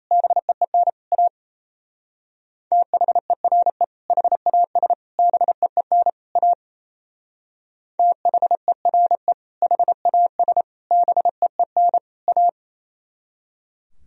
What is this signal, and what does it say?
Morse code, heard as a keyed tone, about 32 wpm, 700 Hz. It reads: BEEN A THERE HAS BEEN A THERE HAS BEEN A
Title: Speed-Racing - Sets of 4 Words 28wpm